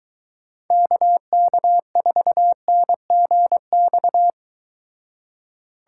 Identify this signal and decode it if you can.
Morse code, heard as a keyed tone, about 22 wpm, 700 Hz. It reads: KK4NGX